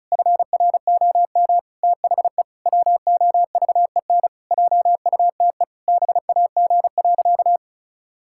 Morse code, heard as a keyed tone, about 36 wpm, 700 Hz. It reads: FROM THE WOVEN JUTE BAG.
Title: Pangram Challenge v2 - 35wpm